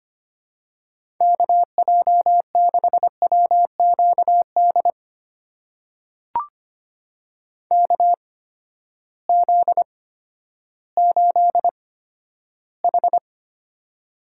Morse code, heard as a keyed tone, about 25 wpm, spaced as for 12 wpm, 700 Hz. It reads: KJ6WQD KZ8H
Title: USA Call Signs - 8x Character Spacing - 25wpm